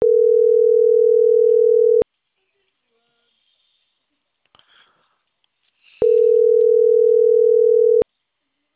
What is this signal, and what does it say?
Ringback tone (North America), 440 + 480 Hz.